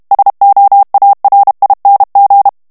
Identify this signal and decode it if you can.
Morse code, heard as a keyed tone, about 32 wpm, 800 Hz. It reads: SOARING
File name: ANGLAIS texte clair 2 vitesse 30 mots minute